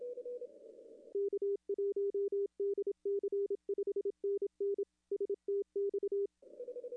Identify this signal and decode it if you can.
Morse code, heard as a keyed tone, about 27 wpm, 410 Hz. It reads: K1DC 5NN STX
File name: contests / arrl-160-k5tr-op-w5tn -0400